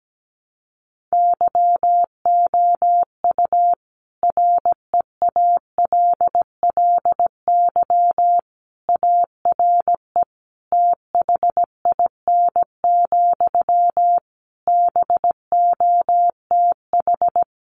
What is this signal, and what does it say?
Morse code, heard as a keyed tone, about 17 wpm, 700 Hz. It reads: YOU REALLY ARE THIN, BOTH
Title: Sentences from Top 1200 Words 17wpm